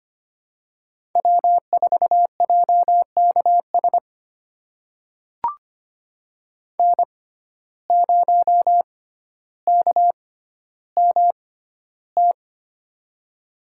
Morse code, heard as a keyed tone, about 25 wpm, spaced as for 13 wpm, 700 Hz. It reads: W4JKS N0KMT
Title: USA Call Signs - 5 Characters - 6x Character Spacing - 25wpm